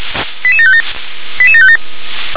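Five-tone selective call (ZVEI), heard as quick runs of five tones, tones of about 70 ms. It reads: 88757 88757
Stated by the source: ZVEI five-tone selective call received on SDR.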